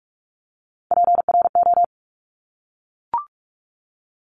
Morse code, heard as a keyed tone, about 46 wpm, 700 Hz. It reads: PRO
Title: Top Three Letter Combinations in Words 45wpm